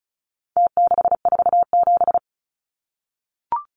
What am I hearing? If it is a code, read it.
Morse code, heard as a keyed tone, about 34 wpm, 700 Hz. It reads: T647
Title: Numbers - Contest Style - 4-Digit 35wpm